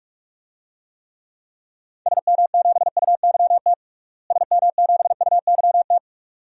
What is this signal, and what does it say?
Morse code, heard as a keyed tone, about 41 wpm, 700 Hz. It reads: SM7UYT SM7UYT
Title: DXCC-100 Call Signs - 6 Characters - Repeated 2x - 45wpm